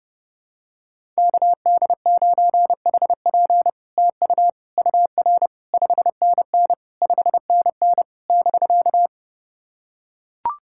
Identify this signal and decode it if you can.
Morse code, heard as a keyed tone, about 30 wpm, 700 Hz. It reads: KD9HP TU UR 5NN 5NN <BK>
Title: SOTA QSO Practice 30wpm